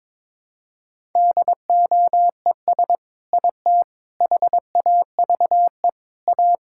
Morse code, heard as a keyed tone, about 22 wpm, 700 Hz. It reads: DOES IT HAVE A
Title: Sentences from Top 1000 Words - Review Part 5 - 22wpm